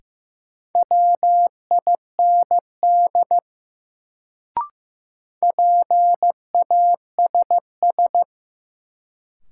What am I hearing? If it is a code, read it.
Morse code, heard as a keyed tone, about 15 wpm, 700 Hz. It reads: WIND PASS
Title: Sentences from Top 400 Words - New Word Drill - 15wpm